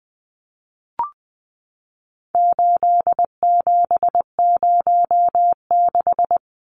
Morse code, heard as a keyed tone, about 19 wpm, 700 Hz. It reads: 8706
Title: Numbers - 4-Digit 20wpm